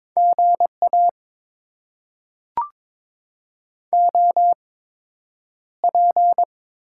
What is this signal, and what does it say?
Morse code, heard as a keyed tone, about 21 wpm, spaced as for 9 wpm, 700 Hz. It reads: GA OP
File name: Top 90 QSO Elements - 8x Character Spacing - 22wpm